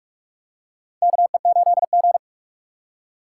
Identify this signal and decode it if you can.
Morse code, heard as a keyed tone, about 45 wpm, 700 Hz. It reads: KE8G
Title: USA Call Signs - 4 Characters - 4x Character Spacing - 45wpm